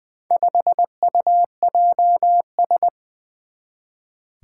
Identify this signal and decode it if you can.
Morse code, heard as a keyed tone, about 20 wpm, 700 Hz. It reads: 5UJS
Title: Four Challenging Letter-Number Combinations 20wpm